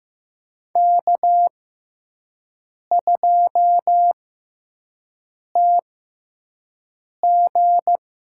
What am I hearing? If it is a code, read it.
Morse code, heard as a keyed tone, about 15 wpm, spaced as for 5 wpm, 700 Hz. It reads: K2TG
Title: USA Call Signs - 4 Characters - 6x Character Spacing - 15wpm